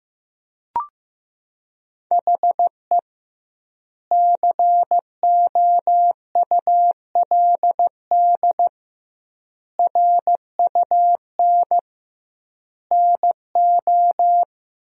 Morse code, heard as a keyed tone, about 16 wpm, 700 Hz. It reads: HE COULD RUN NO
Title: 2x Word-Spacing - Sentences from Top 300 Words - 15wpm